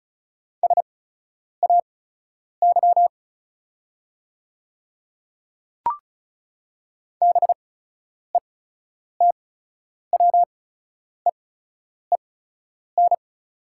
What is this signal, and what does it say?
Morse code, heard as a keyed tone, about 34 wpm, spaced as for 10 wpm, 700 Hz. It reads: SAY BETWEEN
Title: 8x Character Spacing - Top 300 Words - 35wpm